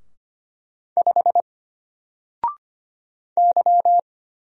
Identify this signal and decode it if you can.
Morse code, heard as a keyed tone, about 25 wpm, 700 Hz. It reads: 5 Y
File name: Single-Letter-Number-25wpm